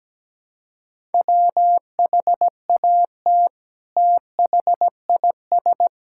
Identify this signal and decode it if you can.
Morse code, heard as a keyed tone, about 17 wpm, 700 Hz. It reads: WHAT THIS
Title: Sentences from Top 1000 Words - Review Part 5 - 17wpm